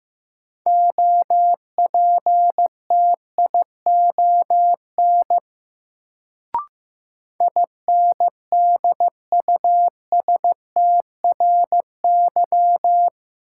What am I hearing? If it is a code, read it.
Morse code, heard as a keyed tone, about 16 wpm, 700 Hz. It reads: OPTION INDUSTRY
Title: Sentences from Top 1300 Words - New Word Drill Alt - 15wpm